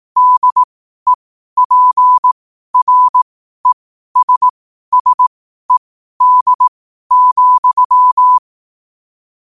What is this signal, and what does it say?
Morse code, heard as a keyed tone, about 18 wpm, spaced as for 12 wpm, 1000 Hz. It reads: DEPRESSED,
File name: qotdfiles / 12wpm